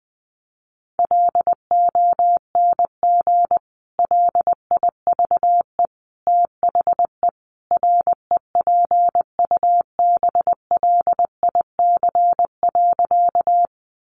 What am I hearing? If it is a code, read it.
Morse code, heard as a keyed tone, about 20 wpm, 700 Hz. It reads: LONG LIVE THE REPUBLIC.
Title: Sentences from Top 1500 Words 20wpm